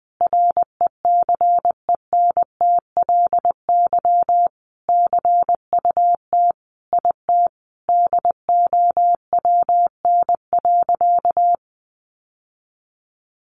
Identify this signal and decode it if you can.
Morse code, heard as a keyed tone, about 20 wpm, 700 Hz. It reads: RECENTLY CUT IT DOWN.
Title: Sentences from Top 1200 Words 20wpm